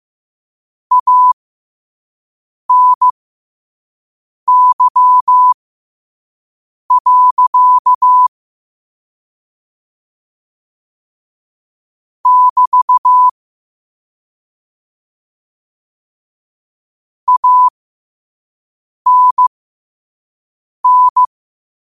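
Morse code, heard as a keyed tone, about 15 wpm, spaced as for 5 wpm, 1000 Hz. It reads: ANY. <BT> ANN
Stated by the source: Morse code sent at 5 words per minute.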